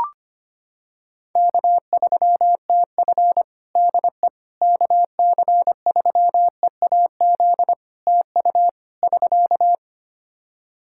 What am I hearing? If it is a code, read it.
Morse code, heard as a keyed tone, about 25 wpm, 700 Hz. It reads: K3TF DE KC3EAZ TU <SK>
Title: Practice QSO - Exchange 4 - Ending - 25wpm